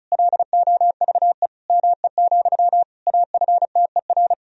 Morse code, heard as a keyed tone, about 36 wpm, 700 Hz. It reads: LOVE ME, AFTER
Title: Sentences from Top 1000 Words - Review Part 1 - 35wpm